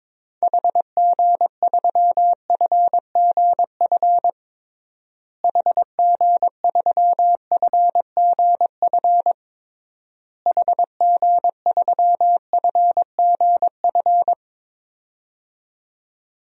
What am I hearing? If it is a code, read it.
Morse code, heard as a keyed tone, about 22 wpm, 700 Hz. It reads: HG3FGF HG3FGF HG3FGF
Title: DXCC-100 Call Signs - 6 Characters - Repeated 3x - 22wpm